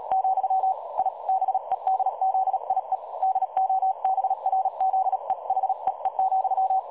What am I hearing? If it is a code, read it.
Morse code, heard as a keyed tone, about 37 wpm, 760 Hz. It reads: , I BEDZIE DOBRZE HEE,
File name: dlazaawansowanych / nagraniazpasma / websdr recording 3565.0kHz